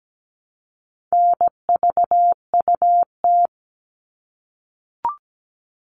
Morse code, heard as a keyed tone, about 17 wpm, 700 Hz. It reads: NVUT
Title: All Cut Numbers - 4-Digit 17wpm